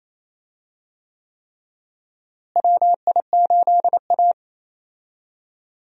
Morse code, heard as a keyed tone, about 28 wpm, 700 Hz. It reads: WI8A
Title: USA Call Signs - 4 Characters - Repeated 3x - 28wpm